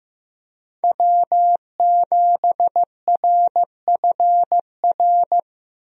Morse code, heard as a keyed tone, about 15 wpm, 700 Hz. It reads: W7RFR
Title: USA Call Signs - 5 Characters - 15wpm